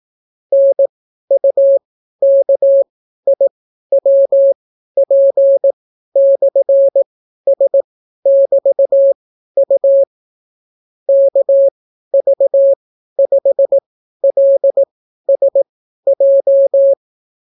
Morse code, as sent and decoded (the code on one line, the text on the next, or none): -. ..- -.- .. .-- .--. -..-. ... -...- ..- / -.- ...- ..... .-.. ... .---
NUKIWP/S<BT>U KV5LSJ